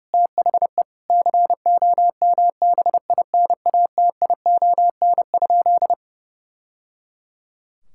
Morse code, heard as a keyed tone, about 30 wpm, 700 Hz. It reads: THE COMBINATION?
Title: Sentences from Top 1000 Words - Review Part 6 - 30wpm